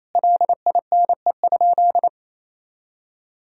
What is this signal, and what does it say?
Morse code, heard as a keyed tone, about 28 wpm, 700 Hz. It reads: LINE?
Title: Sentences from Top 1000 Words - Review Part 1 - 28wpm